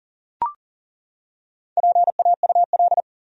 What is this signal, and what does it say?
Morse code, heard as a keyed tone, about 40 wpm, 700 Hz. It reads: PAUL
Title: USA Common Names - Max 4 Letters - 40wpm